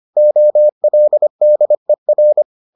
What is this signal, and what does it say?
Morse code, heard as a keyed tone, about 25 wpm, 600 Hz. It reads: OLDER